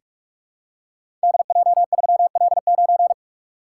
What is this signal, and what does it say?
Morse code, heard as a keyed tone, about 46 wpm, 700 Hz. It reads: DJ3L9
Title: Code Groups - 5 Letter:Numbers - 45wpm